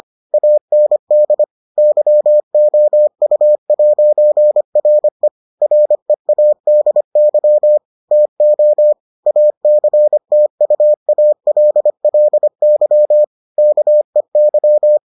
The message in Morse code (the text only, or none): AND YOU'RE READY TO ACTUALLY KEY